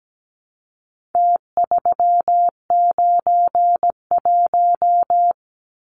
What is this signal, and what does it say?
Morse code, heard as a keyed tone, about 17 wpm, 700 Hz. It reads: T391
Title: Numbers - Contest Style - 4-Digit 17wpm